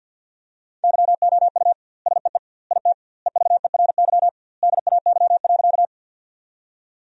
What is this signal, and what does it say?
Morse code, heard as a keyed tone, about 50 wpm, 700 Hz. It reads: YOU SEE IT EVERY DAY.
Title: 2x Word-Spacing - Sentences from Top 200 Words - 50wpm